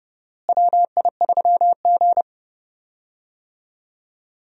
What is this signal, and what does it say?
Morse code, heard as a keyed tone, about 30 wpm, 700 Hz. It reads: WI3G
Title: USA Call Signs - 4 Characters - Repeated 3x - 30wpm